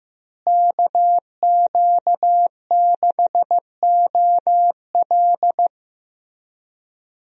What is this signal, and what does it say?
Morse code, heard as a keyed tone, about 15 wpm, 700 Hz. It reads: KQ6OL
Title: USA Call Signs - 5 Characters - Repeated 3x - 15wpm